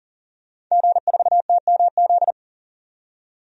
Morse code, heard as a keyed tone, about 40 wpm, 700 Hz. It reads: G4TMZ
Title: DXCC-100 Call Signs - 5 Characters - 40wpm